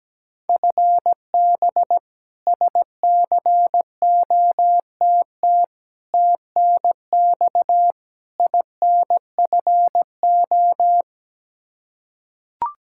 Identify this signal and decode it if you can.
Morse code, heard as a keyed tone, about 17 wpm, 700 Hz. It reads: FB SCOTT TNX INFO
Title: K1USN SST QSO Practice 17wpm